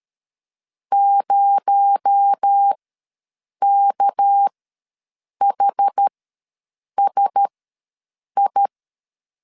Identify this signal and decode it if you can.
Morse code, heard as a keyed tone, about 13 wpm, spaced as for 7 wpm, 800 Hz. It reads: 0KHSI
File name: LEZIONI / 6 PKX0 / 6 PKX0 ws 12wpm 3448